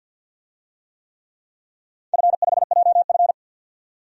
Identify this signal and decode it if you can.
Morse code, heard as a keyed tone, about 50 wpm, 700 Hz. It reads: F5JF
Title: DXCC-100 Call Signs - 4 Characters - 50wpm